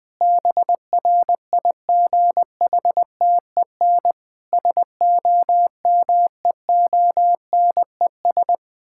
Morse code, heard as a keyed tone, about 20 wpm, 700 Hz. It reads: BRIGHTEN SOMEONES
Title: Difficult Sentences with Hints - 20wpm